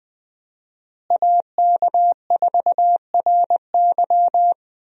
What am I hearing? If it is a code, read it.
Morse code, heard as a keyed tone, about 20 wpm, 700 Hz. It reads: AK4RY
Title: USA Call Signs - 5 Characters - Repeated 3x - 20wpm